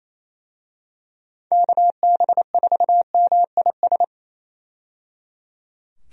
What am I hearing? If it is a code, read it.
Morse code, heard as a keyed tone, about 28 wpm, 700 Hz. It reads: KB4MIS
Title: USA Call Signs - 6 Characters - 28wpm